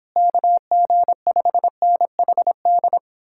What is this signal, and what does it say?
Morse code, heard as a keyed tone, about 31 wpm, 700 Hz. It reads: KG5NHD